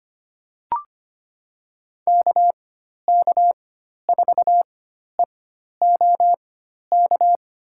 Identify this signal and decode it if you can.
Morse code, heard as a keyed tone, about 25 wpm, spaced as for 12 wpm, 700 Hz. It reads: KK4EOK
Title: USA Call Signs - 4x Character Spacing - 25wpm